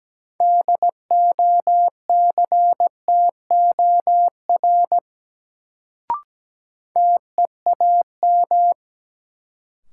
Morse code, heard as a keyed tone, about 17 wpm, 700 Hz. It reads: DOCTOR TEAM
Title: Sentences from Top 1500 Words - New Word Drill Alt - 17wpm